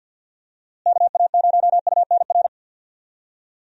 Morse code, heard as a keyed tone, about 49 wpm, 700 Hz. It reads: KA0UNR
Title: USA Call Signs - 6 Characters - Repeated 2x - 50wpm